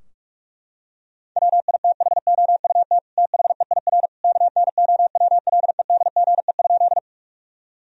Morse code, heard as a keyed tone, about 46 wpm, 700 Hz. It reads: WITHOUT THEIR KNOWLEDGE?
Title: Sentences from Top 1000 Words - Review Part 3 - 45wpm